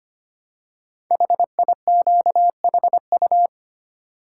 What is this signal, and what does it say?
Morse code, heard as a keyed tone, about 25 wpm, 700 Hz. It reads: HIQHU
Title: Code Groups - 5 Letters - 25wpm